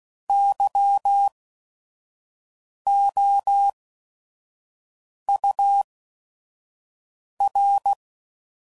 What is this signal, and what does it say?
Morse code, heard as a keyed tone, about 16 wpm, spaced as for 5 wpm, 800 Hz. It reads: YOUR
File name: CW Code Groups @5WPM / QSOs / kc7sac